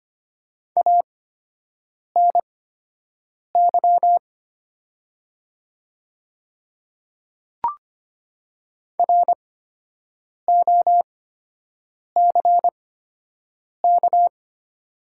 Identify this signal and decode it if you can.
Morse code, heard as a keyed tone, about 25 wpm, spaced as for 7 wpm, 700 Hz. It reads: ANY ROCK
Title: 8x Character Spacing - Top 500 Words - 25wpm